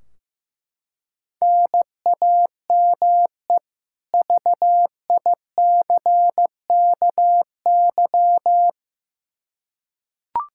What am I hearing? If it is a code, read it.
Morse code, heard as a keyed tone, about 15 wpm, 700 Hz. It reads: NAME VICKY
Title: Practice QSO - Exchange 1 - Beginning - 15wpm